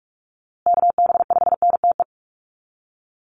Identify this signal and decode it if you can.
Morse code, heard as a keyed tone, about 44 wpm, 700 Hz. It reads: KB5NTE
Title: USA Call Signs Repeated 3x 45wpm